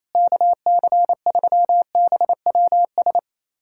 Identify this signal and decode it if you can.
Morse code, heard as a keyed tone, about 28 wpm, 700 Hz. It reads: KC3BWS
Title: USA Callsign DE Callsign 28wpm